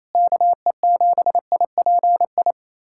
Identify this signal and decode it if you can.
Morse code, heard as a keyed tone, about 28 wpm, 700 Hz. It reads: KE7IPI
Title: USA Callsign DE Callsign - 6 Characters - 28wpm